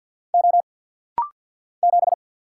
Morse code, heard as a keyed tone, about 50 wpm, 700 Hz. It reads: O 7
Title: Single Letter-Number Alt - Rapid-Fire 50wpm